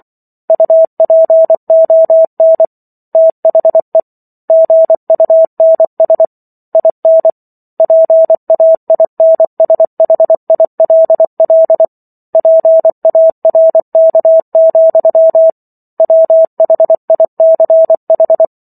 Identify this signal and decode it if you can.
Morse code, heard as a keyed tone, about 24 wpm, 650 Hz. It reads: UPON THE GUNS IN PAINSHILL PARK, WHICH